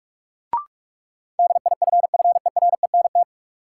Morse code, heard as a keyed tone, about 46 wpm, 700 Hz. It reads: DIFFERENT
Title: Sentences from Top 600 Words - New Word Drill - 45wpm